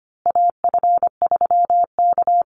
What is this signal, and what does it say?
Morse code, heard as a keyed tone, about 25 wpm, 700 Hz. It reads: AF3K
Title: Call Signs Repeated 3x - 25wpm